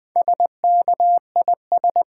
Morse code, heard as a keyed tone, about 20 wpm, 700 Hz. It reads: SKIS
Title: Easy Words with Hints - 20wpm